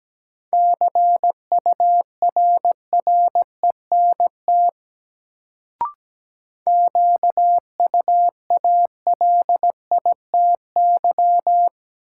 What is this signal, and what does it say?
Morse code, heard as a keyed tone, about 17 wpm, 700 Hz. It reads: CURRENT QUALITY
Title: Sentences from Top 1000 Words - Review Part 4 - New Word Drill - 17wpm